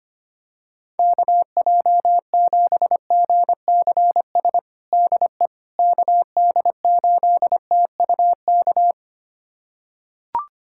Morse code, heard as a keyed tone, about 25 wpm, 700 Hz. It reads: KJ7GCS DE KD8TUK
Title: USA Callsign DE Callsign 25wpm